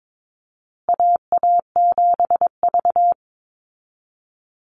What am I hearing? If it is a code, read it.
Morse code, heard as a keyed tone, about 22 wpm, 700 Hz. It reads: AA7V
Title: Call Signs Repeated 2x - 22wpm